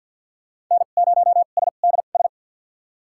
Morse code, heard as a keyed tone, about 47 wpm, spaced as for 35 wpm, 690 Hz. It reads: N0SDS
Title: USA Call Signs - 5 Characters - 2x Character Spacing - 50wpm